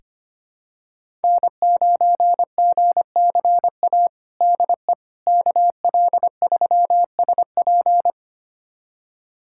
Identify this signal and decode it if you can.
Morse code, heard as a keyed tone, about 25 wpm, 700 Hz. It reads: N9GCA DE KL3SP
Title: USA Callsign DE Callsign - 5 Characters - 25wpm